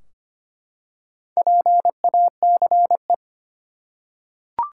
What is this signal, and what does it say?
Morse code, heard as a keyed tone, about 25 wpm, 700 Hz. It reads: PACE
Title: Four Letter Words - 25wpm